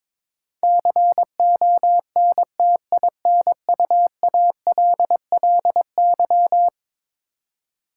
Morse code, heard as a keyed tone, about 22 wpm, 700 Hz. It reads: CONTINUALLY
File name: practice / 22wpm / Eleven Letter Words - 22wpm